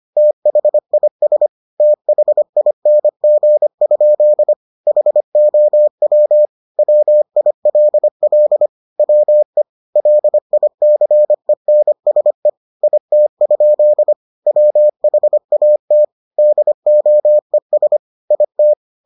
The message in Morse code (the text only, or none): THIS THING? HOW WILL WE LICENSE IT? WHAT DOES IT